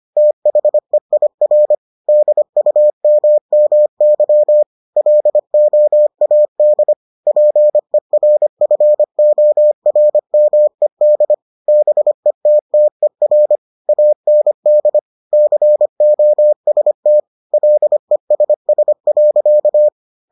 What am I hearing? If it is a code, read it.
Morse code, heard as a keyed tone, about 25 wpm, 600 Hz. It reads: THEIR DUMMY LOAD PERFORMED BETTER AND COST LESS.